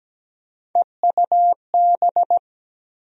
Morse code, heard as a keyed tone, about 17 wpm, 700 Hz. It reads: EUB
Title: USA Call Sign Postfixes - 3 Letters 17wpm